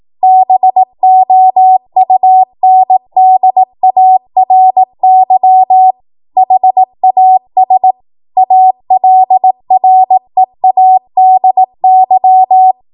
Morse code, heard as a keyed tone, about 18 wpm, 750 Hz. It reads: BOUNDARY HAS ALREADY
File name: Morse / 18WPM